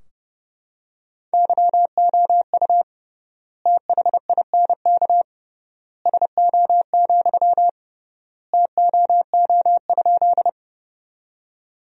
Morse code, heard as a keyed tone, about 32 wpm, 700 Hz. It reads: YOU THINK SO, TOO?
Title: 3x Word-Spacing - Sentences from Top 200 Words - 30wpm